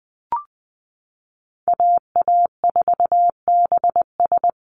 Morse code, heard as a keyed tone, about 20 wpm, 700 Hz. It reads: AA4BS
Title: USA Call Signs 20wpm